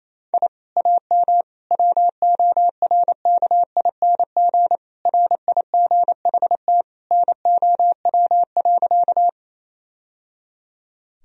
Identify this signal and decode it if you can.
Morse code, heard as a keyed tone, about 28 wpm, 700 Hz. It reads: I AM WORKING RIGHT NOW.
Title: Sentences from Top 1000 Words - Review Part 4 - 28wpm